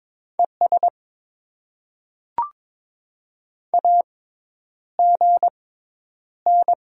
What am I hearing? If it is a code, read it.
Morse code, heard as a keyed tone, about 21 wpm, spaced as for 10 wpm, 700 Hz. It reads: ES AGN
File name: Top 20 QSO Elements - 6x Character Spacing - 22wpm